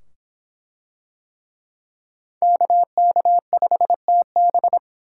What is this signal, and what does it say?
Morse code, heard as a keyed tone, about 23 wpm, 700 Hz. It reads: KK5TB